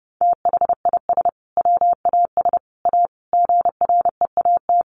Morse code, heard as a keyed tone, about 30 wpm, 700 Hz. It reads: THIS WAS A GREAT
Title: Sentences from Top 1400 Words 30wpm